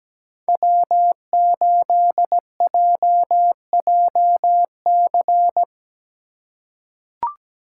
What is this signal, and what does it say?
Morse code, heard as a keyed tone, about 17 wpm, 700 Hz. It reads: W8JJC
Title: USA Call Signs - 5 Characters - 8x Character Spacing - 17wpm